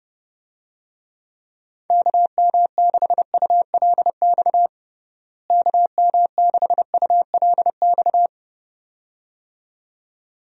Morse code, heard as a keyed tone, about 32 wpm, 700 Hz. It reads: KM6ULX KM6ULX
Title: USA Call Signs - 6 Characters - Repeated 2x - 30wpm